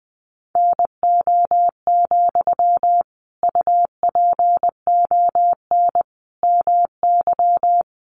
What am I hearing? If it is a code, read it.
Morse code, heard as a keyed tone, about 20 wpm, 700 Hz. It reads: NO, UPON MY
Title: Sentences from Top 1100 Words 20wpm